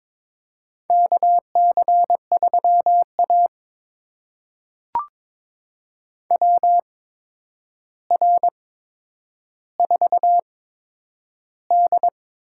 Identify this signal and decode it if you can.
Morse code, heard as a keyed tone, about 22 wpm, spaced as for 9 wpm, 700 Hz. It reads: KC3A WR4D
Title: USA Call Signs - 4 Characters - 8x Character Spacing - 22wpm